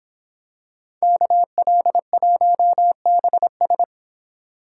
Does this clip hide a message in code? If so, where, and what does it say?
Yes: Morse code from 1.0 to 3.8 s, heard as a keyed tone, about 26 wpm, 700 Hz. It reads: KL1BS